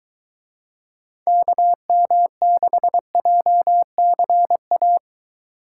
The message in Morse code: KM6JCA

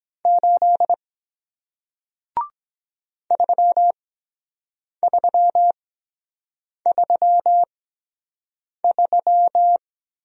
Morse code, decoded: 8 3333

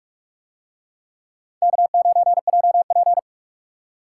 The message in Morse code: K9JP

